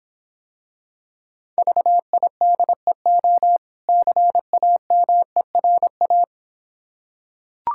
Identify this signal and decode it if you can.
Morse code, heard as a keyed tone, about 26 wpm, 690 Hz. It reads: VIDEO CAMERA